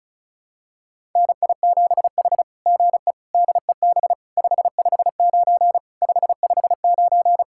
Morse code, heard as a keyed tone, about 33 wpm, 700 Hz. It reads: NI7H GE DEB 559 559
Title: SOTA QSO Practice 35wpm